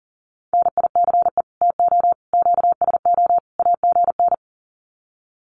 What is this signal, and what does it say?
Morse code, heard as a keyed tone, about 40 wpm, 700 Hz. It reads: NICE TO QSO AGN
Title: Common QSO Phrases 40wpm